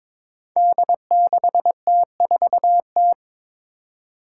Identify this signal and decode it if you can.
Morse code, heard as a keyed tone, about 22 wpm, 700 Hz. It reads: D6T4T
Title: Five Challenging Letter-Number Combinations 22wpm